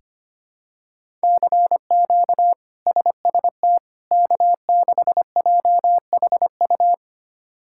Morse code, heard as a keyed tone, about 25 wpm, 700 Hz. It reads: CQ SST K6JHU
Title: K1USN SST QSO Practice - 25wpm